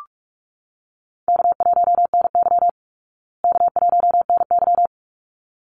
Morse code, heard as a keyed tone, about 44 wpm, 700 Hz. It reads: K1NY K1NY
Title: Easy Call Signs Repeated 2x - 45wpm